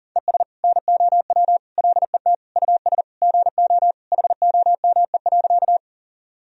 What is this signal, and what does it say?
Morse code, heard as a keyed tone, about 40 wpm, 700 Hz. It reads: ES NOW LET US GO HOME.
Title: Sentences from Top 300 Words and Abbreviations 40wpm